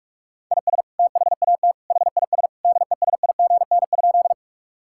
Morse code, heard as a keyed tone, about 46 wpm, 700 Hz. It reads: IS THAT HIS DESIGN?
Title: Sentences from Top 1000 Words - Review Part 5 - 45wpm